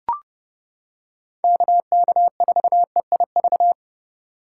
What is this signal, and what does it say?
Morse code, heard as a keyed tone, about 29 wpm, 700 Hz. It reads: KK4EIV